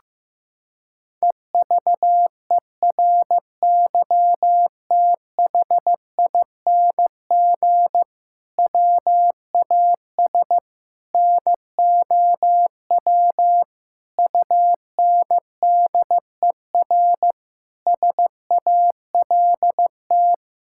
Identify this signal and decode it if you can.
Morse code, heard as a keyed tone, about 15 wpm, 700 Hz. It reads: EVERYTHING WAS NOW UNDER SALT
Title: Sentences from Top 1000 Words - Review Part 5 - 15wpm